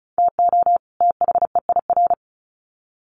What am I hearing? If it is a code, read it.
Morse code, heard as a keyed tone, about 34 wpm, 700 Hz. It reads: TO THEIR
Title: Sets of 2 Words 35wpm